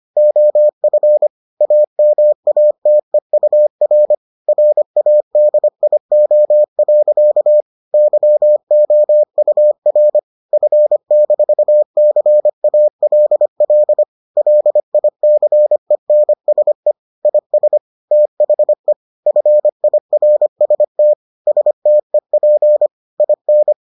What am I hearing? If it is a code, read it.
Morse code, heard as a keyed tone, about 25 wpm, 600 Hz. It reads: OF AMATEUR RADIO. YOUR F-CALL LICENSE IS THE FIRST STEP IN